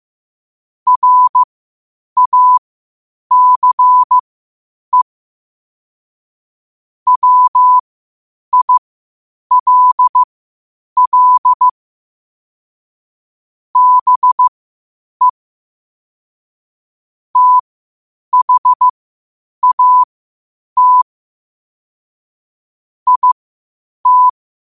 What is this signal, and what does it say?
Morse code, heard as a keyed tone, about 15 wpm, spaced as for 8 wpm, 1000 Hz. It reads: RACE WILL BE THAT IT
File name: qotdfiles / 8wpm